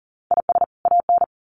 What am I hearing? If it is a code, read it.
Morse code, heard as a keyed tone, about 40 wpm, 700 Hz. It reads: IS AN
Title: Sets of 2 Words 40wpm